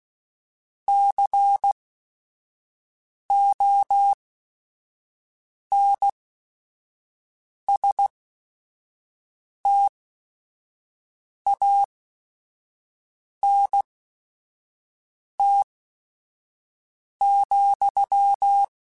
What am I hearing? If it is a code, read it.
Morse code, heard as a keyed tone, about 16 wpm, spaced as for 5 wpm, 800 Hz. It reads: CONSTANT,
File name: Technical Subjects / Technical Subjects-mp3s / XC @5